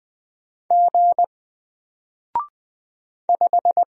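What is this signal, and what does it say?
Morse code, heard as a keyed tone, about 20 wpm, 700 Hz. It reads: G 5
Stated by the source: Single letter and number Morse drill at 20 wpm, from N4LCD.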